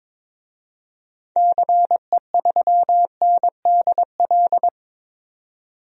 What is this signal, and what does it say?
Morse code, heard as a keyed tone, about 22 wpm, 700 Hz. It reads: CE3NDL
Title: DXCC-100 Call Signs - 6 Characters - 22wpm